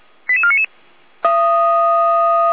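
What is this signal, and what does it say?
Five-tone selective call (ZVEI), heard as quick runs of five tones, tones of about 70 ms. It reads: 88499
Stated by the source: ZVEI five-tone call 88190 received on an SDR.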